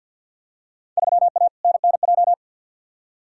Morse code, heard as a keyed tone, about 47 wpm, 700 Hz. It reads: 3A NNJ
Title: Field Day Exchanges - 50wpm